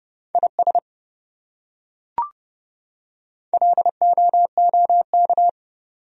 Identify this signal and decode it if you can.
Morse code, heard as a keyed tone, about 28 wpm, 700 Hz. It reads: IS LOOK